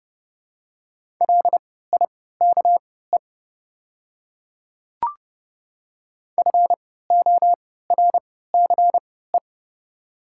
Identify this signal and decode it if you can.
Morse code, heard as a keyed tone, about 29 wpm, spaced as for 17 wpm, 700 Hz. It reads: LIKE FORCE
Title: 3x Character Spacing - Top 500 Words - 30wpm